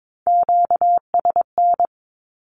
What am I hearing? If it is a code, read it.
Morse code, heard as a keyed tone, about 22 wpm, 700 Hz. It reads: QSN
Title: Q-Codes with Spoken Meaning - 22wpm